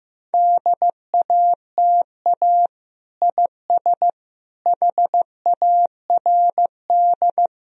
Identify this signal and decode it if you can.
Morse code, heard as a keyed tone, about 15 wpm, 700 Hz. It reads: DATA IS HARD
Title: Sentences from Top 1000 Words - Review Part 3 - 15wpm